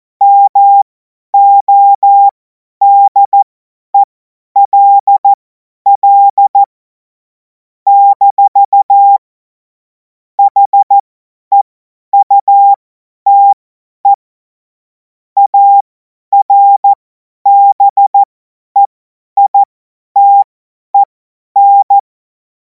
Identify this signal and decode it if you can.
Morse code, heard as a keyed tone, about 14 wpm, spaced as for 10 wpm, 800 Hz. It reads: MODELL - HEUTE ARBEITEN
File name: Morsekurs / Witze / Witz 14. April 2026 50 F70